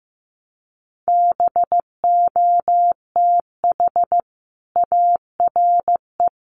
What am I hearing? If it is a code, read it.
Morse code, heard as a keyed tone, about 15 wpm, 700 Hz. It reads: BOTH ARE
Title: Sentences from Top 1200 Words 15wpm